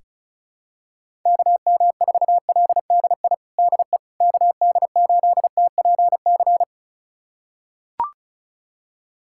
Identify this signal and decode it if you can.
Morse code, heard as a keyed tone, about 36 wpm, 700 Hz. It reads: KM4LDI DE KD8TPC
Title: USA Callsign DE Callsign - 6 Characters - 35wpm